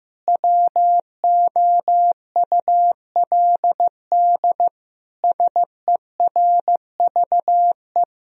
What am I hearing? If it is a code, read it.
Morse code, heard as a keyed tone, about 15 wpm, 700 Hz. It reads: WOULD SERVE
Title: Sentences from Top 1000 Words - Review Part 2 - 15wpm